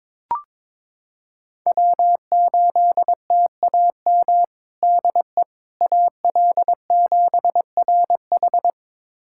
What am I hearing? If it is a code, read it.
Morse code, heard as a keyed tone, about 22 wpm, 700 Hz. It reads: W8TAM DE AL7RH
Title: USA Callsign DE Callsign - 5 Characters - 22wpm